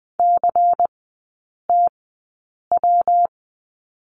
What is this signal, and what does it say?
Morse code, heard as a keyed tone, about 20 wpm, spaced as for 8 wpm, 700 Hz. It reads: CTW
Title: 114 Lesson 19e - Letter w - Three Characters 20wpm